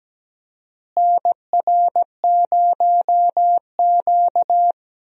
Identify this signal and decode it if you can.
Morse code, heard as a keyed tone, about 17 wpm, 700 Hz. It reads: NR0Q
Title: USA Call Signs - 4 Characters - Repeated 2x - 17wpm